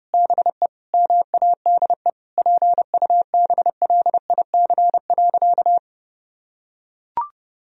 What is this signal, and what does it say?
Morse code, heard as a keyed tone, about 30 wpm, 700 Hz. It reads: BE MADE PUBLIC.
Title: Sentences from Top 1000 Words - Review Part 3 - 30wpm